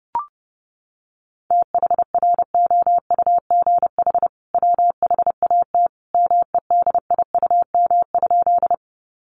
Morse code, heard as a keyed tone, about 30 wpm, 700 Hz. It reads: THROUGH WHAT MEDIUM?
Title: Sentences from Top 900 Words 30wpm